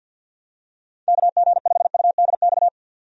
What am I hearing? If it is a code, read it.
Morse code, heard as a keyed tone, about 52 wpm, 700 Hz. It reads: KG5UDX
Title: USA Call Signs - 6 Characters - 50wpm